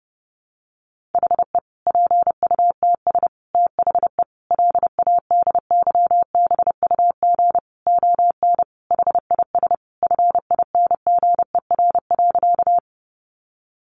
Morse code, heard as a keyed tone, about 30 wpm, 700 Hz. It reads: HE PUTS THE LADYBUG ON HIS FINGER.
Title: Additional Easy Phrases 30wpm